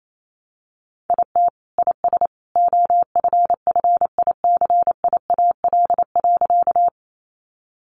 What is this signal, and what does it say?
Morse code, heard as a keyed tone, about 28 wpm, 700 Hz. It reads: IT IS OFFICIAL.
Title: Sentences from Top 1000 Words 28wpm